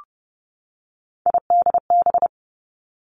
Morse code, heard as a keyed tone, about 30 wpm, 700 Hz. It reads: IDB
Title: Three Challenging Letter-Number Combinations 30wpm